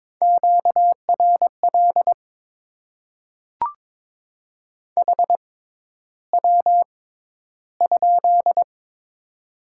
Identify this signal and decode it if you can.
Morse code, heard as a keyed tone, about 22 wpm, spaced as for 11 wpm, 700 Hz. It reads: QRL HW?
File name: Top 20 QSO Elements - 6x Character Spacing - 22wpm